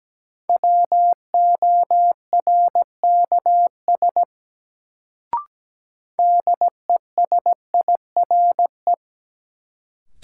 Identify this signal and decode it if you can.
Morse code, heard as a keyed tone, about 17 wpm, 700 Hz. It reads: WORKS DESIRE
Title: Sentences from Top 1100 Words - New Word Drill Alt - 17wpm